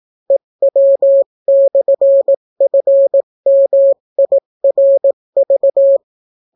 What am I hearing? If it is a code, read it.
Morse code, heard as a keyed tone, about 18 wpm, 550 Hz. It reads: EW/FMIRV